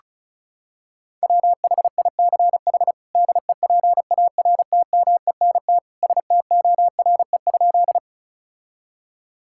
Morse code, heard as a keyed tone, about 36 wpm, 700 Hz. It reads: WHICH DEPARTMENT STORE?
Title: Sentences from Top 1000 Words - Review Part 5 - 35wpm